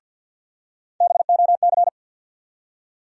Morse code, heard as a keyed tone, about 53 wpm, 700 Hz. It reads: BOC
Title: USA Call Sign Postfixes - 3 Letters 50wpm